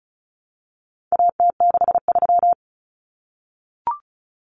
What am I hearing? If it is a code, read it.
Morse code, heard as a keyed tone, about 34 wpm, 700 Hz. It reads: AT63
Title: Common Cut Numbers - 4-Digit 35wpm